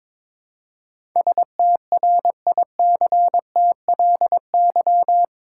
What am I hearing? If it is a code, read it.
Morse code, heard as a keyed tone, about 22 wpm, 700 Hz. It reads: STRICTLY
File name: practice / 22wpm / Eight Letter Words - 22wpm